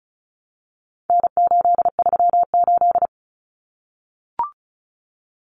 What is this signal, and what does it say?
Morse code, heard as a keyed tone, about 34 wpm, 700 Hz. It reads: N838
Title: Common Cut Numbers - 4-Digit 35wpm